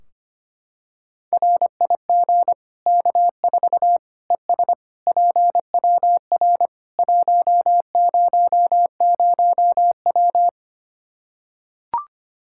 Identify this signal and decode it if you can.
Morse code, heard as a keyed tone, about 25 wpm, 700 Hz. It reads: RIG K4 ES PWR 100W
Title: Practice QSO - Exchange 2 - Info - 25wpm